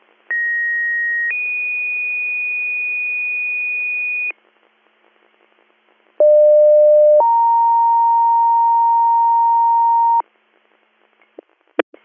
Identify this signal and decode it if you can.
Two-tone page, heard as a one-second tone, then a three-second tone, about 1900 then 2450 Hz, about 600 then 930 Hz. It is heard from 0.3 to 10.2 s.